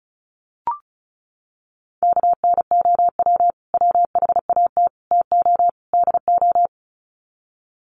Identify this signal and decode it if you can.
Morse code, heard as a keyed tone, about 34 wpm, 700 Hz. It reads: KNOW WHAT TO DO
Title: Sets of 4 Words 35wpm